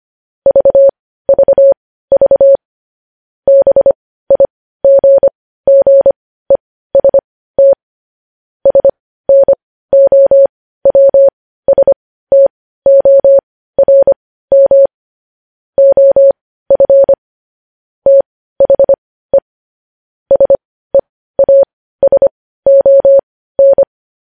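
Morse code, heard as a keyed tone, about 25 wpm, spaced as for 15 wpm, 550 Hz. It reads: VVV BIGGEST SNOWSTORM OF THE SEASON
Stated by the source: Morse practice sending news headlines at 15 wpm.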